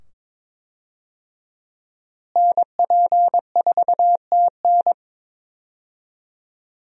Morse code, heard as a keyed tone, about 22 wpm, 700 Hz. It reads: NP4TN
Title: USA Call Signs - 5 Characters - Repeated 2x - 22wpm